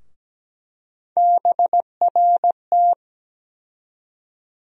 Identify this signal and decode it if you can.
Morse code, heard as a keyed tone, about 17 wpm, 700 Hz. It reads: BRT
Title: USA Call Sign Postfixes - 3 Letters 17wpm